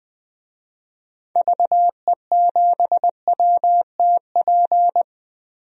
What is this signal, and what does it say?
Morse code, heard as a keyed tone, about 20 wpm, 700 Hz. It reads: VE7WTP
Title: DXCC-100 Call Signs - 6 Characters - Repeated 3x - 20wpm